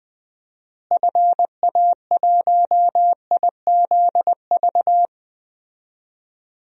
Morse code, heard as a keyed tone, about 20 wpm, 700 Hz. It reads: FA1IZV
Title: DXCC-100 Call Signs - 6 Characters - Repeated 3x - 20wpm